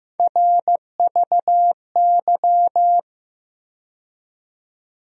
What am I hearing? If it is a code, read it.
Morse code, heard as a keyed tone, about 15 wpm, 700 Hz. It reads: RVY
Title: USA Call Sign Postfixes - 3 Letters 15wpm